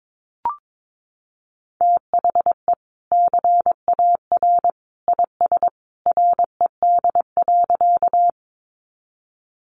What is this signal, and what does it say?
Morse code, heard as a keyed tone, about 22 wpm, 700 Hz. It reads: THE CAR IS RED.
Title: Easy Phrases 22wpm